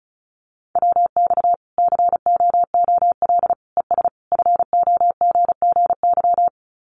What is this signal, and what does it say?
Morse code, heard as a keyed tone, about 34 wpm, 700 Hz. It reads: WX COOL ES FOGGY
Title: Common QSO Phrases 35wpm